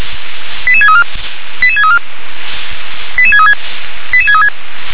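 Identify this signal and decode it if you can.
Five-tone selective call (ZVEI), heard as quick runs of five tones, tones of about 70 ms. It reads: 88634 88634 88636 88636
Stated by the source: ZVEI five-tone selective call received by SDR.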